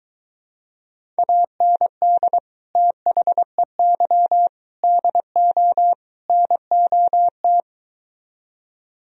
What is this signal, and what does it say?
Morse code, heard as a keyed tone, about 23 wpm, 700 Hz. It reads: AND THEY DO NOT